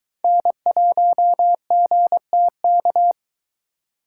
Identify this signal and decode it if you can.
Morse code, heard as a keyed tone, about 22 wpm, 700 Hz. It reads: N1GTK